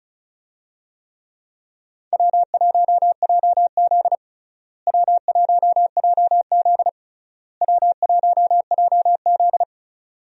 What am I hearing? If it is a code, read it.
Morse code, heard as a keyed tone, about 36 wpm, 700 Hz. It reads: W1JZ W1JZ W1JZ
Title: USA Call Signs - 4 Characters - Repeated 3x - 35wpm